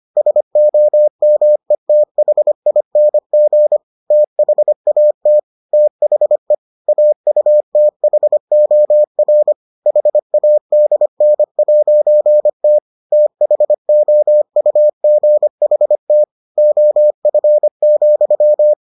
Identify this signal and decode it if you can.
Morse code, heard as a keyed tone, about 25 wpm, 600 Hz. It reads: SOMETHING THAT THE AUTHOR HADN'T THOUGHT OF,